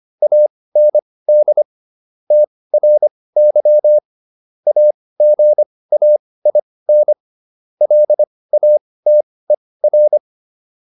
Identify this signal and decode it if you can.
Morse code, heard as a keyed tone, about 25 wpm, spaced as for 18 wpm, 600 Hz. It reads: AND TRY AGAIN LATER